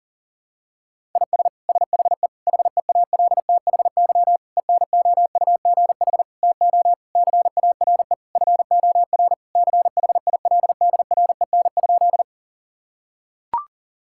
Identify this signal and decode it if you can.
Morse code, heard as a keyed tone, about 40 wpm, 700 Hz. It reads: IS SHE HEALTHY ENOUGH TO CARE FOR CHILDREN?
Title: Sentences from Top 1000 Words - Review Part 5 - 40wpm